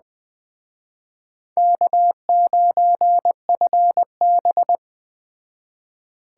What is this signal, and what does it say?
Morse code, heard as a keyed tone, about 21 wpm, 700 Hz. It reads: K9FB